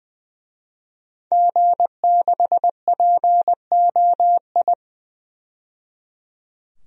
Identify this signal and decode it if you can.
Morse code, heard as a keyed tone, about 20 wpm, 700 Hz. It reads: G6POI
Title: DXCC-100 Call Signs - 5 Characters - 20wpm